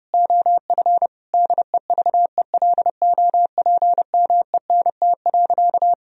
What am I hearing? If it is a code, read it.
Morse code, heard as a keyed tone, about 30 wpm, 700 Hz. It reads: OF DEVELOPMENT.
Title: Sentences from Top 1000 Words - Review Part 4 - 30wpm